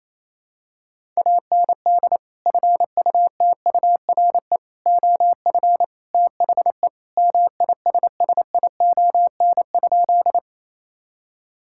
Morse code, heard as a keyed tone, about 28 wpm, 700 Hz. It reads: AND FUTURE OF THE MISSION?
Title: Sentences from Top 1600 Words 28wpm